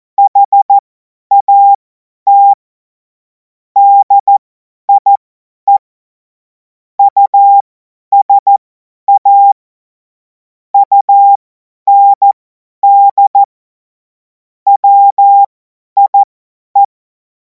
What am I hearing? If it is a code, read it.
Morse code, heard as a keyed tone, about 14 wpm, spaced as for 10 wpm, 800 Hz. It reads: HAT DIE USA UND WIE
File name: Morsekurs / Witze / Witz 12. Mai 2026 50 F70